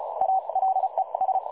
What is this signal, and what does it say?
Morse code, heard as a keyed tone, about 36 wpm, spaced as for 30 wpm, 760 Hz, with radio noise. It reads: ALES
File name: dlazaawansowanych / nagraniazpasma / websdr recording 3565.0kHz